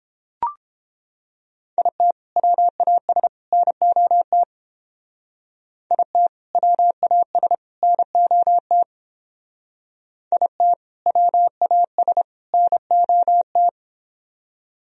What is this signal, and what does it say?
Morse code, heard as a keyed tone, about 28 wpm, 700 Hz. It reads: IT WAS NOT IT WAS NOT IT WAS NOT